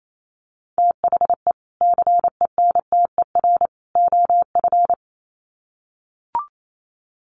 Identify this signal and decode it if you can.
Morse code, heard as a keyed tone, about 28 wpm, 700 Hz. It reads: THE CENTER OF
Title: Sets of 3 Words 28wpm